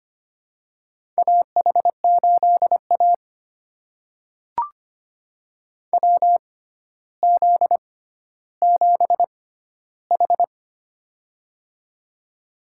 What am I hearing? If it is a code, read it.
Morse code, heard as a keyed tone, about 25 wpm, spaced as for 13 wpm, 700 Hz. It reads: AH8A WZ7H
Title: USA Call Signs - 4 Characters - 6x Character Spacing - 25wpm